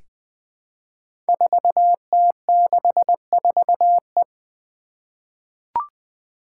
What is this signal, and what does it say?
Morse code, heard as a keyed tone, about 20 wpm, 700 Hz. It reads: 4T64E
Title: Five Challenging Letter-Number Combinations 20wpm